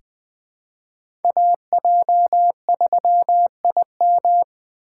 Morse code, heard as a keyed tone, about 20 wpm, 700 Hz. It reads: AJ3IM
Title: USA Call Signs - 5 Characters - 3x Character Spacing - 20wpm